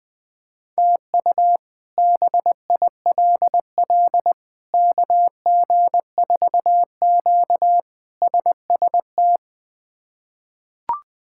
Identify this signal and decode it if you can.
Morse code, heard as a keyed tone, about 20 wpm, 700 Hz. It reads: TU BILL KG4Q SST
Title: K1USN SST QSO Practice - 20wpm